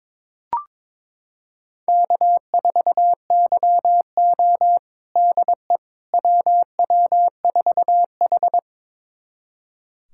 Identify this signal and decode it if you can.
Morse code, heard as a keyed tone, about 22 wpm, 700 Hz. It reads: K4YO DE WW4H
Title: USA Callsign DE Callsign - 4 Characters - 22wpm